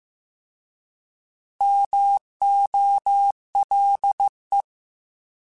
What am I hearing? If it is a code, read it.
Morse code, heard as a keyed tone, about 15 wpm, 800 Hz. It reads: MOLE